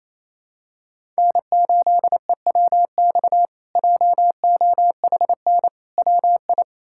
Morse code, heard as a keyed tone, about 28 wpm, 700 Hz. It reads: N8EWX JOHN WI
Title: K1USN SST QSO Practice - 28wpm